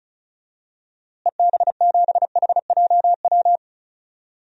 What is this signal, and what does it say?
Morse code, heard as a keyed tone, about 36 wpm, 700 Hz. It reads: EB7HJW
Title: DXCC-100 Call Signs - 6 Characters - Repeated 2x - 35wpm